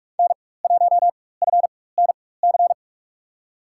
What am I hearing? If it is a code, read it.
Morse code, heard as a keyed tone, about 42 wpm, spaced as for 20 wpm, 700 Hz. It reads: N1FNC
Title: USA Call Signs - 5 Characters - 4x Character Spacing - 45wpm